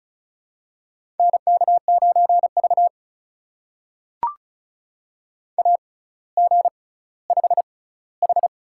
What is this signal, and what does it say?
Morse code, heard as a keyed tone, about 36 wpm, spaced as for 18 wpm, 690 Hz. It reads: NK9V AG5H
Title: USA Call Signs - 4 Characters - 6x Character Spacing - 35wpm